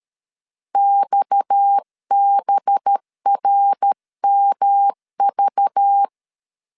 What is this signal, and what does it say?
Morse code, heard as a keyed tone, about 13 wpm, 800 Hz. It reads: XBRMV
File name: LEZIONI / 7 GF9Z / 7 G 12wpm 2055